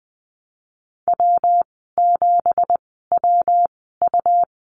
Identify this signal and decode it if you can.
Morse code, heard as a keyed tone, about 20 wpm, spaced as for 15 wpm, 700 Hz. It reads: W7WU
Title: 167 Lesson 27g - Number 7 - Call Signs 20wpm